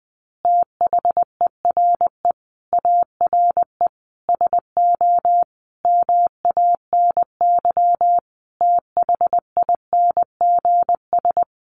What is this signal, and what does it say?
Morse code, heard as a keyed tone, about 20 wpm, 700 Hz. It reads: THERE ARE SO MANY THINGS
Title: Additional Easy Phrases 20wpm